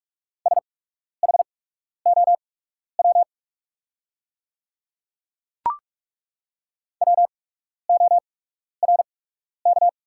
Morse code, heard as a keyed tone, about 42 wpm, spaced as for 12 wpm, 700 Hz. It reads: SHOW WORK
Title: 8x Character Spacing - Top 200 Words - 45wpm